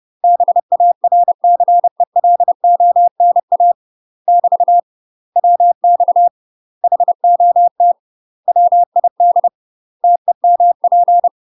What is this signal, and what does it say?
Morse code, heard as a keyed tone, about 30 wpm, 700 Hz. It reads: BARCELONA <BT> WX HOT WID TEMP
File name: UK MASTER QSOs TRAINING / 30WPM / UK MASTER QSOs 0030